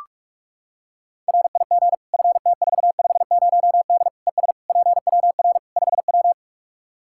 Morse code, heard as a keyed tone, about 46 wpm, 700 Hz. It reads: RIG FT450D ES PWR 5W
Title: Practice QSO - Exchange 2 - Info - 45wpm